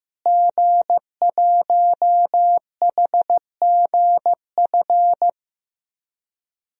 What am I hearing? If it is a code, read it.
Morse code, heard as a keyed tone, about 15 wpm, 700 Hz. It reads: G1HGF
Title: DXCC-100 Call Signs - 5 Characters - Repeated 2x - 15wpm